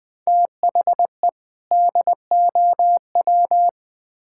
Morse code, heard as a keyed tone, about 20 wpm, 700 Hz. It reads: THE DOW
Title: Build A Phrase - 5-Word Phrases - 20wpm